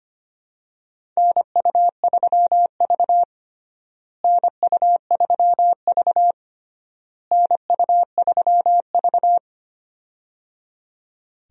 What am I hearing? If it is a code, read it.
Morse code, heard as a keyed tone, about 25 wpm, 700 Hz. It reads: NU3V NU3V NU3V
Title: USA Call Signs - 4 Characters - Repeated 3x - 25wpm